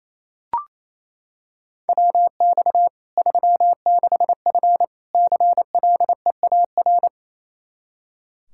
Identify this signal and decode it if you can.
Morse code, heard as a keyed tone, about 28 wpm, 700 Hz. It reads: WX 36F CLEAR
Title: Practice QSO - Exchange 2 - Info - 28wpm